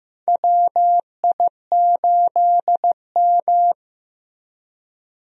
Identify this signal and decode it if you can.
Morse code, heard as a keyed tone, about 16 wpm, 700 Hz. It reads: WI8M
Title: USA Call Signs - 4 Characters - Repeated 2x - 15wpm